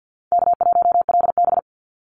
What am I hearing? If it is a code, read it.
Morse code, heard as a keyed tone, about 50 wpm, 700 Hz. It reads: K1LB
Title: NA SOTA Activator Callsigns 50wpm